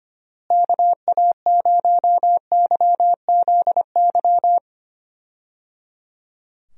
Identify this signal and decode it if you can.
Morse code, heard as a keyed tone, about 25 wpm, 700 Hz. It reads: KA0YZY
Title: USA Call Signs - 6 Characters - Repeated 3x - 25wpm